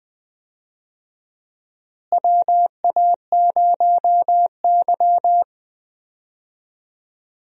Morse code, heard as a keyed tone, about 20 wpm, 700 Hz. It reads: WA0Y
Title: USA Call Signs - 4 Characters - Repeated 3x - 20wpm